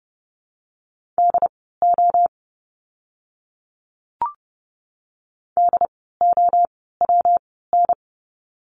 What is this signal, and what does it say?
Morse code, heard as a keyed tone, about 30 wpm, spaced as for 17 wpm, 700 Hz. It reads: DO DOWN
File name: course / 30:10wpm / 237 Lesson 38f - Letter q - Words 30wpm